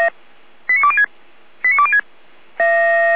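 Five-tone selective call (ZVEI), heard as quick runs of five tones, tones of about 80 ms. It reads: 89206 89206
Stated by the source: Received on an SDR.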